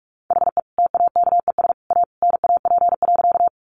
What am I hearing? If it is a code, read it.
Morse code, heard as a keyed tone, about 44 wpm, 700 Hz. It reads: HE TAKES A NAP.
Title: Easy Phrases 45wpm